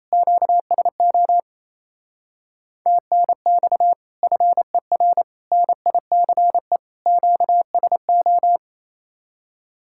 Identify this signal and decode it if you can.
Morse code, heard as a keyed tone, about 36 wpm, 700 Hz. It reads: QSO TNX FER NICE QSO